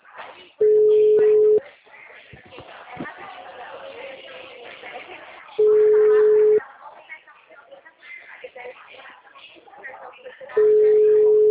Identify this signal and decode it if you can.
Ringback tone (Europe, 425 Hz), with speech.